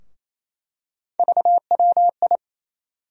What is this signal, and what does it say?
Morse code, heard as a keyed tone, about 28 wpm, 700 Hz. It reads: VWI
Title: USA Call Sign Postfixes - 3 Letters 28wpm